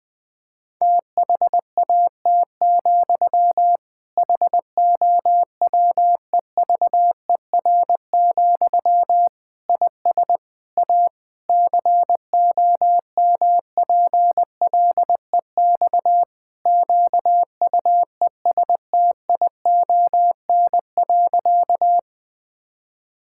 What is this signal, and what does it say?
Morse code, heard as a keyed tone, about 20 wpm, 700 Hz. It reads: THAT, HOWEVER, IS A COMPLEX QUESTION.
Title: Sentences from Top 1000 Words - Review Part 6 - 20wpm